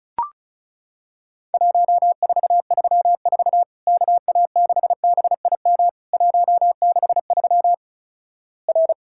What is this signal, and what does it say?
Morse code, heard as a keyed tone, about 36 wpm, 700 Hz. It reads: 1434 KA6BIM 163 R
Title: Worked All Europe DX Contest - QTC Exchange - 35wpm